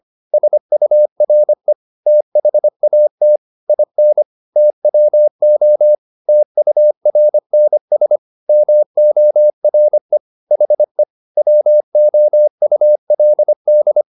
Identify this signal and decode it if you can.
Morse code, heard as a keyed tone, about 25 wpm, 600 Hz. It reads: SURE THAT IN TWO TURNS MORE HE WOULD